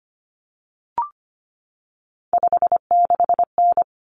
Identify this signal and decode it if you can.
Morse code, heard as a keyed tone, about 25 wpm, 700 Hz. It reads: 56N
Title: Common Cut Numbers - 3-Digit 25wpm